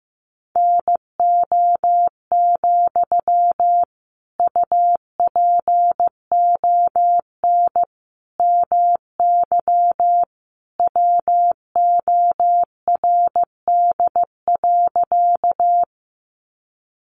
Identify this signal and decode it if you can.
Morse code, heard as a keyed tone, about 15 wpm, 700 Hz. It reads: NO, UPON MY WORD.
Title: Sentences from Top 1100 Words 15wpm